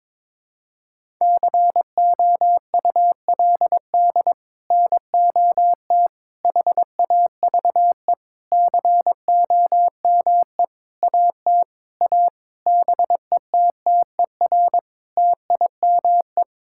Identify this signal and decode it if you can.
Morse code, heard as a keyed tone, about 22 wpm, 700 Hz. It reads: COULD NOT HAVE COME AT A BETTER TIME
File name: practice / 22wpm / Sets of 8 Words - 22wpm